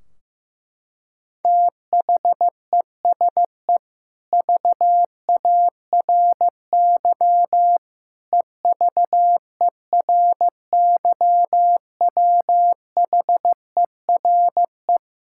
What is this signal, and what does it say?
Morse code, heard as a keyed tone, about 15 wpm, 700 Hz. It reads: THESE VARY EVERYWHERE
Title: Sentences from Top 1600 Words 15wpm